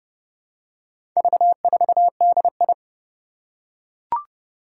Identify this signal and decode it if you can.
Morse code, heard as a keyed tone, about 30 wpm, 700 Hz. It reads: V4DI
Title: Four Challenging Letter-Number Combinations 30wpm